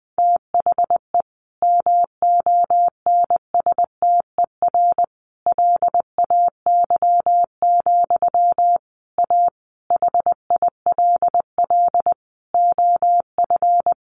Morse code, heard as a keyed tone, about 20 wpm, 700 Hz. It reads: THE MONSTER LAY, A HILL OF
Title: A Sound Of Thunder 20wpm